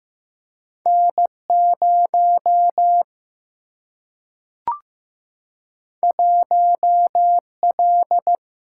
Morse code, heard as a keyed tone, about 15 wpm, 700 Hz. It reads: N0 1L
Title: Single-Letter Single-Number 15wpm